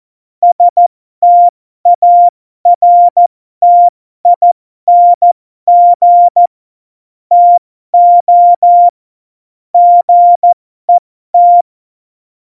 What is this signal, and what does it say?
Morse code, heard as a keyed tone, about 14 wpm, spaced as for 12 wpm, 700 Hz. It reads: STARTING TO GET
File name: Morsekurs / Witze